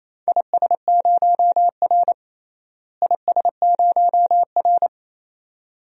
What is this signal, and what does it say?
Morse code, heard as a keyed tone, about 28 wpm, 700 Hz. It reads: IS0R IS0R
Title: DXCC-100 Call Signs - 4 Characters - Repeated 2x - 28wpm